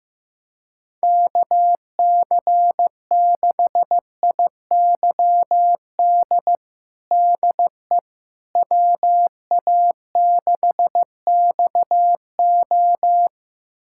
Morse code, heard as a keyed tone, about 15 wpm, 700 Hz. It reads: KC6IYD DE WA6XO
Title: USA Callsign DE Callsign 15wpm